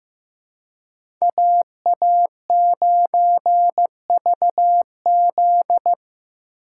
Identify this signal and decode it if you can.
Morse code, heard as a keyed tone, about 15 wpm, 700 Hz. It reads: AA9VZ
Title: USA Call Signs - 5 Characters - 2x Character Spacing - 15wpm